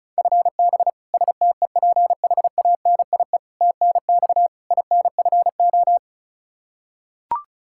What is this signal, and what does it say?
Morse code, heard as a keyed tone, about 36 wpm, 700 Hz. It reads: FB STEPHANIE TNX INFO
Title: Practice QSO - Exchange 3 - More Info - 35wpm